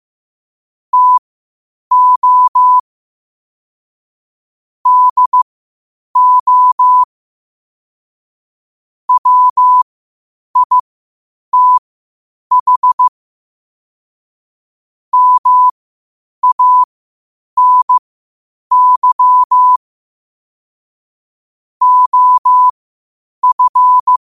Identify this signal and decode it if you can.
Morse code, heard as a keyed tone, about 15 wpm, spaced as for 8 wpm, 1000 Hz. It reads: TO DO WITH MANY OF
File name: qotdfiles / 8wpm